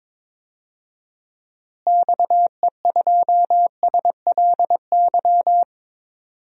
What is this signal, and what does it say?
Morse code, heard as a keyed tone, about 22 wpm, 700 Hz. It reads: XE2SLY
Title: DXCC-100 Call Signs - 6 Characters - Repeated 3x - 22wpm